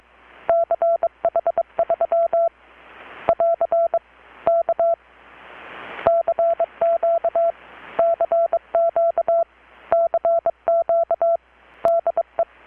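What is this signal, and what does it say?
Morse code, heard as a keyed tone, about 23 wpm, 670 Hz. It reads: CH3 <AR> K CQ CQ CQ DE
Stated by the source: Morse from KPH's Night of Nights XX maritime radiotelegraphy event, received in Chiapas, Mexico about 3,800 km away on a Kenwood R-600 with a grounded 10 m long-wire antenna.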